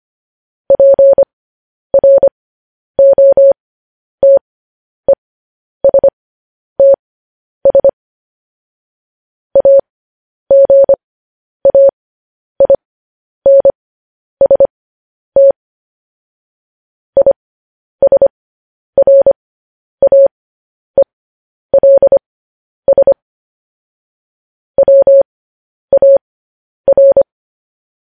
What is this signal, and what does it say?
Morse code, heard as a keyed tone, about 25 wpm, spaced as for 10 wpm, 550 Hz. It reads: PROTESTS AGAINST ISRAELS WAR